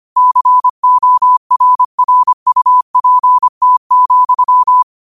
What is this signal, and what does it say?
Morse code, heard as a keyed tone, about 25 wpm, 1000 Hz. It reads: CORRUPT,
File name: qotdfiles / 25wpm